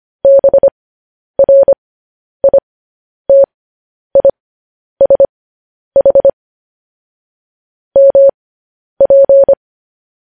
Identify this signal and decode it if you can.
Morse code, heard as a keyed tone, about 25 wpm, spaced as for 10 wpm, 550 Hz. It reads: BRITISH MP
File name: podcast / News headlines-1x-25x10